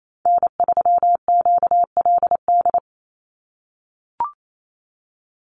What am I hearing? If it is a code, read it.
Morse code, heard as a keyed tone, about 28 wpm, 700 Hz. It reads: N3QLD
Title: USA Call Signs 28wpm